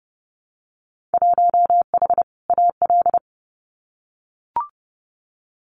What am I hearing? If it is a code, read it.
Morse code, heard as a keyed tone, about 30 wpm, 700 Hz. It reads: 1H AL
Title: Winter Field Day Exchanges 30wpm